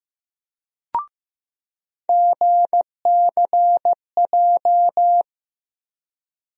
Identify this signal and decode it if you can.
Morse code, heard as a keyed tone, about 15 wpm, 700 Hz. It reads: GCJ
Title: USA Call Sign Postfixes - 3 Letters 15wpm